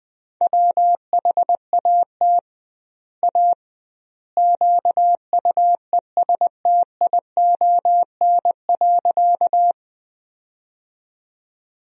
Morse code, heard as a keyed tone, about 20 wpm, 700 Hz. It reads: WHAT A QUESTION.
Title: 2x Word-Spacing - Sentences from Top 400 Words - 20wpm